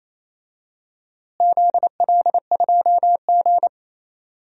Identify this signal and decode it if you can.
Morse code, heard as a keyed tone, about 28 wpm, 700 Hz. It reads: ZL2G
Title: DXCC-100 Call Signs - 4 Characters - Repeated 3x - 28wpm